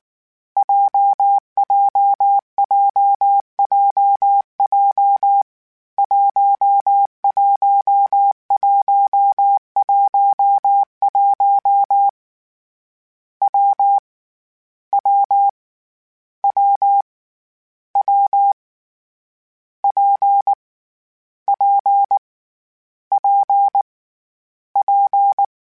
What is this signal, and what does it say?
Morse code, heard as a keyed tone, about 19 wpm, 800 Hz. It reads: JJJJJ 11111 W W W W P P P P
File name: Difficult Combos / Difficult Combos-mp3s / WPJ WPJ1 @18